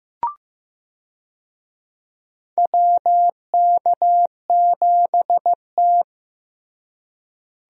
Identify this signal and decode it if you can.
Morse code, heard as a keyed tone, about 15 wpm, 700 Hz. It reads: WK7T
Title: USA Call Signs - 4 Characters - Repeated 2x - 15wpm